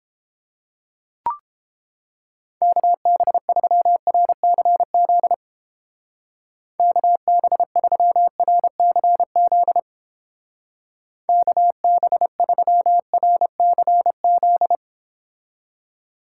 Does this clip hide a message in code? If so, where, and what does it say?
Yes: Morse code from 2.6 to 14.8 s, heard as a keyed tone, about 29 wpm, 700 Hz. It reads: KB3RCZ KB3RCZ KB3RCZ